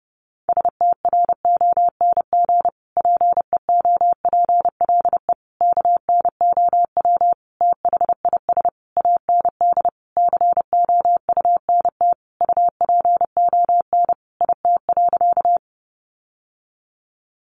Morse code, heard as a keyed tone, about 30 wpm, 700 Hz. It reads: STRONG PEOPLE KNOW THIS AND COUNT UPON IT.
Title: Sentences from Top 1500 Words 30wpm